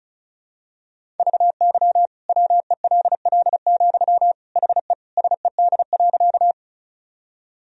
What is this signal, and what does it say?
Morse code, heard as a keyed tone, about 34 wpm, 700 Hz. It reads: VY WELL, HE SED.
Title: Sentences from Top 200 Words and Abbreviations 35wpm